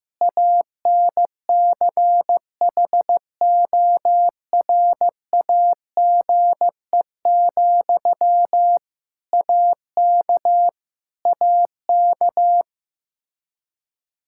Morse code, heard as a keyed tone, about 15 wpm, 700 Hz. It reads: ANCHORAGE, AK AK
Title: Practice QSO - Exchange 1 - Beginning - 15wpm